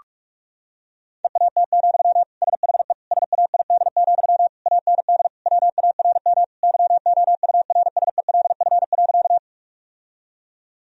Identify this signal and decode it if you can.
Morse code, heard as a keyed tone, about 46 wpm, 700 Hz. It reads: EAT, SHE SAID, AND WARM YOURSELF.
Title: Sentences from Top 1000 Words - Review Part 2 - 45wpm